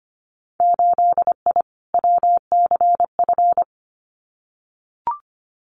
Morse code, heard as a keyed tone, about 25 wpm, 700 Hz. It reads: 8I WCF
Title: Winter Field Day Exchanges Encore 25wpm